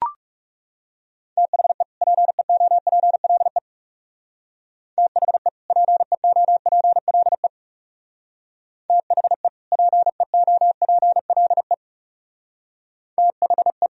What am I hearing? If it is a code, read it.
Morse code, heard as a keyed tone, about 36 wpm, 700 Hz. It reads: THE PEOPLE THE PEOPLE THE PEOPLE THE
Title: Speed-Racing - Sets of 2 Words 30wpm